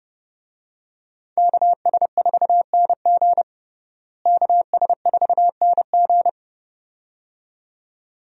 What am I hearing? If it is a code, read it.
Morse code, heard as a keyed tone, about 30 wpm, 700 Hz. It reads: KS4NG KS4NG
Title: USA Call Signs - 5 Characters - Repeated 2x - 30wpm